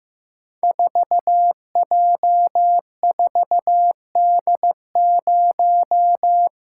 Morse code, heard as a keyed tone, about 15 wpm, 700 Hz. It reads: 4J4D0
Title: Code Groups - 5 Letter:Numbers - 15wpm